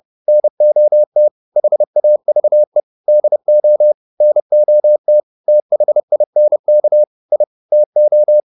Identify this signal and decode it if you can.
Morse code, heard as a keyed tone, about 30 wpm, 600 Hz. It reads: NOT HAVE DO NOT THINK I TO
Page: advanced-practice-files